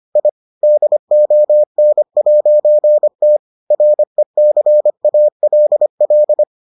Morse code, heard as a keyed tone, about 25 wpm, 600 Hz. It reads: I DON'T RECALL